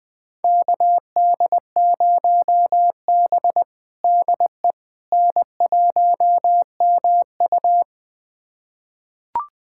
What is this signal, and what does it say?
Morse code, heard as a keyed tone, about 20 wpm, 700 Hz. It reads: KD0B DE N1MU
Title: USA Callsign DE Callsign - 4 Characters - 20wpm